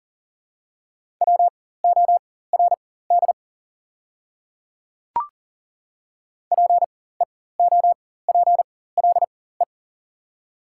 Morse code, heard as a keyed tone, about 40 wpm, spaced as for 19 wpm, 700 Hz. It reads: WORD PEOPLE
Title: 4x Character Spacing - Top 300 Words - 40wpm